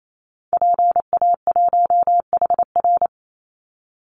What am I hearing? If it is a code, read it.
Morse code, heard as a keyed tone, about 28 wpm, 700 Hz. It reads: PA1HR
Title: Call Signs Repeated 2x - 28wpm